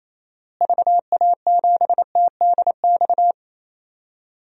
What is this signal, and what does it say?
Morse code, heard as a keyed tone, about 28 wpm, 700 Hz. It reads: VA7TDX
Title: DXCC-100 Call Signs - 6 Characters - Repeated 2x - 28wpm